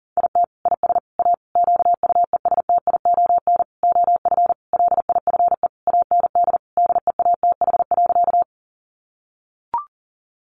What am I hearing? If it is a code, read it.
Morse code, heard as a keyed tone, about 40 wpm, 700 Hz. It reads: IT IS A QUESTION OF LIFE AND DEATH.
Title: Sentences from Top 1100 Words 40wpm